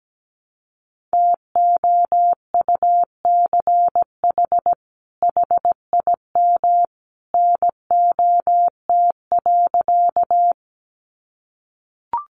Morse code, heard as a keyed tone, about 17 wpm, 700 Hz. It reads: TOUCH HIM NOT.
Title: Sentences from Top 1200 Words 17wpm